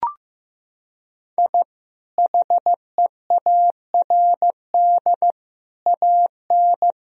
Morse code, heard as a keyed tone, about 15 wpm, 700 Hz. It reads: I HEARD AN
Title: Sentences from Top 1000 Words - Review Part 4 - 15wpm